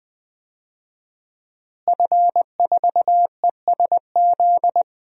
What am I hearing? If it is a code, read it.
Morse code, heard as a keyed tone, about 20 wpm, 700 Hz. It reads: F4ESZ
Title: DXCC-100 Call Signs - 5 Characters - Repeated 2x - 20wpm